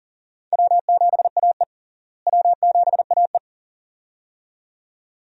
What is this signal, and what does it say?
Morse code, heard as a keyed tone, about 40 wpm, 700 Hz. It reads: W7AE W7AE
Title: USA Call Signs - 4 Characters - Repeated 2x - 40wpm